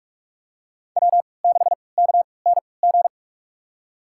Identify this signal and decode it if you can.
Morse code, heard as a keyed tone, about 46 wpm, spaced as for 25 wpm, 700 Hz. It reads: W6KNG
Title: USA Call Signs - 3x Character Spacing - 45wpm